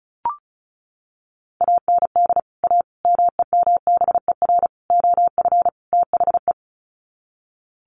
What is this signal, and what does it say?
Morse code, heard as a keyed tone, about 34 wpm, 700 Hz. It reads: AND A MEMBER OF THE
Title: Sets of 5 Words 35wpm